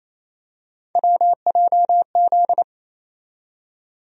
Morse code, heard as a keyed tone, about 28 wpm, 700 Hz. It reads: WJZ
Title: USA Call Sign Postfixes - 3 Letters 28wpm